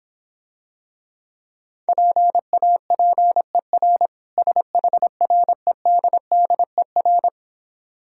Morse code, heard as a keyed tone, about 26 wpm, 700 Hz. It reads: PAPER SHREDDER